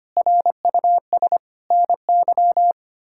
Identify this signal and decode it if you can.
Morse code, heard as a keyed tone, about 25 wpm, 700 Hz. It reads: RUS NY
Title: K1USN SST QSO Practice - 25wpm